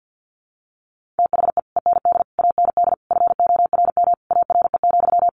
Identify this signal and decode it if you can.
Morse code, heard as a keyed tone, about 50 wpm, 700 Hz. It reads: THE END AND FORM ARE,
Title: Sentences from Top 1400 Words 50wpm